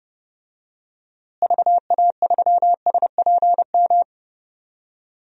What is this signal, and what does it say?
Morse code, heard as a keyed tone, about 31 wpm, 700 Hz. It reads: VA3SPM
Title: DXCC-100 Call Signs - 6 Characters - Repeated 3x - 30wpm